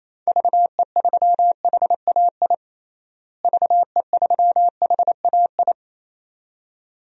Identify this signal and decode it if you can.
Morse code, heard as a keyed tone, about 28 wpm, 700 Hz. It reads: VE3HAI VE3HAI
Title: DXCC-100 Call Signs - 6 Characters - Repeated 2x - 28wpm